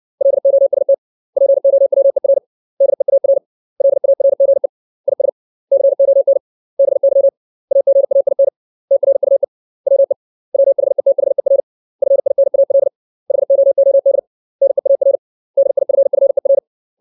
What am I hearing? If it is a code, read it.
Morse code, heard as a keyed tone, about 60 wpm, 550 Hz. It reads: LOST POWER DEAR LARGE IS CON BY AGAIN TRUE WE WHETHER FINAL HOOD NEAR DIFFER
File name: 632 words / 632 Words 60WPM